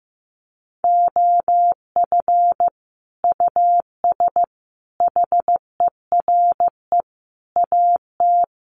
Morse code, heard as a keyed tone, about 15 wpm, 700 Hz. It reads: OF US HERE AT
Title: Sets of 5 Words 15wpm